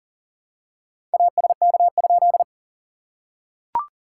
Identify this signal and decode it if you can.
Morse code, heard as a keyed tone, about 37 wpm, 690 Hz. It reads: ASK?
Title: SOTA QSO Practice 40wpm